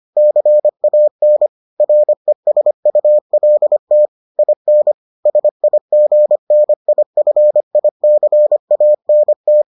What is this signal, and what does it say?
Morse code, heard as a keyed tone, about 25 wpm, 600 Hz. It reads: CAN RESULT IN SIGNIFICANT